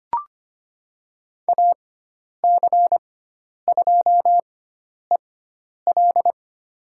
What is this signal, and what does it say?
Morse code, heard as a keyed tone, about 25 wpm, spaced as for 10 wpm, 700 Hz. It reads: AC2EL
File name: Lesson 16g - Letter c - Call Signs 25wpm